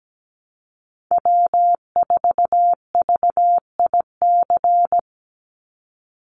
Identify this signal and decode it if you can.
Morse code, heard as a keyed tone, about 17 wpm, 700 Hz. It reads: W4VIC
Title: Call Signs Repeated 2x - 17wpm